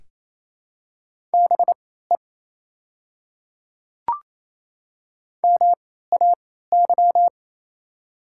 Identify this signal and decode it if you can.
Morse code, heard as a keyed tone, about 28 wpm, spaced as for 16 wpm, 700 Hz. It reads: BE MAY
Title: 3x Character Spacing - Top 100 Words - 28wpm